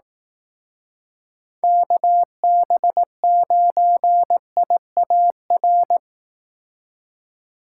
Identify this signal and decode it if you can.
Morse code, heard as a keyed tone, about 18 wpm, 700 Hz. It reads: KB9IAR
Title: Speed Racing - USA Call Signs 15wpm